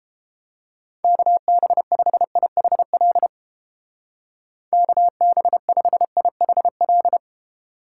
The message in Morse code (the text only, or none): KB5IHL KB5IHL